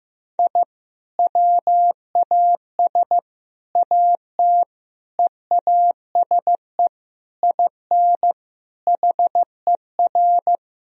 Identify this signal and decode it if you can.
Morse code, heard as a keyed tone, about 15 wpm, 700 Hz. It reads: I WAS AT EASE IN HER
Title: Sentences from Top 1000 Words - Review Part 1 - 15wpm